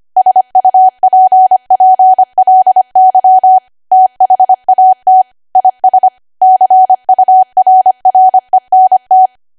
Morse code, heard as a keyed tone, about 25 wpm, 750 Hz. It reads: SUPPLY THAT IS CURRENT
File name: cwa / W1AW-Level3 / 25WPM